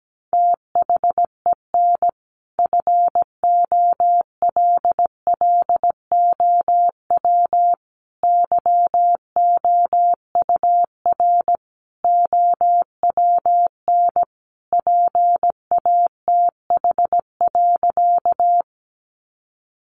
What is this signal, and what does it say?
Morse code, heard as a keyed tone, about 17 wpm, 700 Hz. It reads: THEN FOLLOW YOUR OWN PATH.
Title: Sentences from Top 1400 Words 17wpm